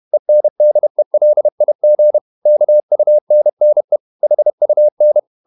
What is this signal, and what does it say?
Morse code, heard as a keyed tone, about 31 wpm, 600 Hz. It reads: ENDELIG KUNNE HUN
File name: Ferie QRQ 31wpm